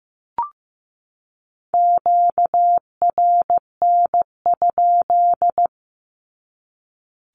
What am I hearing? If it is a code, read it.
Morse code, heard as a keyed tone, about 15 wpm, 700 Hz. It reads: QRN?
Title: Q-Codes 15wpm